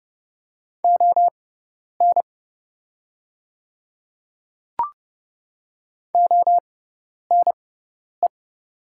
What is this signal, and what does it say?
Morse code, heard as a keyed tone, about 30 wpm, spaced as for 10 wpm, 690 Hz. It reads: ON ONE
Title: 6x Character Spacing - Top 100 Words - 30wpm